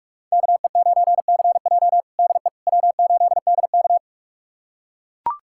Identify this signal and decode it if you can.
Morse code, heard as a keyed tone, about 41 wpm, 700 Hz. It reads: KE9CJ DE W8DK
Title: USA Callsign DE Callsign 45wpm